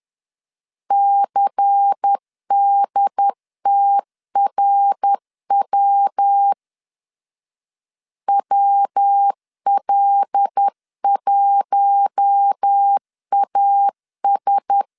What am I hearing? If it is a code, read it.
Morse code, heard as a keyed tone, about 10 wpm, 800 Hz. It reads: CDTRW WL1AS
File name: LEZIONI / 6 PKX0 / 6 P 10wpm 2432